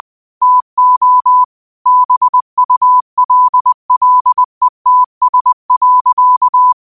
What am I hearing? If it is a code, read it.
Morse code, heard as a keyed tone, about 20 wpm, 1000 Hz. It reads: TO BULLETS.
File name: qotdfiles / 20wpm